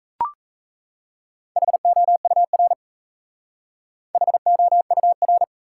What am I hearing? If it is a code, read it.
Morse code, heard as a keyed tone, about 36 wpm, 700 Hz. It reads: HOUR HOUR